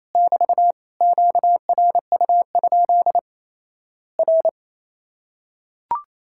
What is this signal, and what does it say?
Morse code, heard as a keyed tone, about 28 wpm, 700 Hz. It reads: <BT> QRU? R
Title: Worked All Europe DX Contest - QTC Exchange - 28wpm